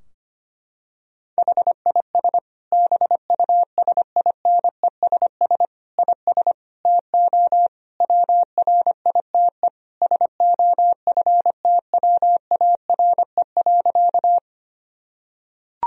Morse code, heard as a keyed tone, about 25 wpm, 700 Hz. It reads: HIS BUSINESS IS TO WRITE SOFTWARE.
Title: Sentences from Top 1000 Words - Review Part 3 - 25wpm